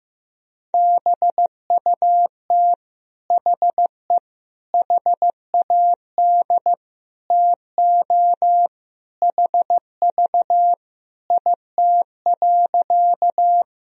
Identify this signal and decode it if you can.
Morse code, heard as a keyed tone, about 15 wpm, 700 Hz. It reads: BUT HE HAD TO HV IT.
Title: Sentences from Top 100 Words and Abbreviations 15wpm